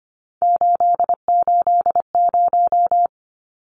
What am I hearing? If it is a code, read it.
Morse code, heard as a keyed tone, about 25 wpm, 700 Hz. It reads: 880
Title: Numbers - 3-Digit 25wpm